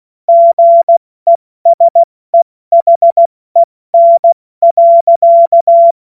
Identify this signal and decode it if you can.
Morse code, heard as a keyed tone, about 16 wpm, 680 Hz. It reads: GESEHEN.
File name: Morsekurs / Witze / Witz 09 April 2024 70 F80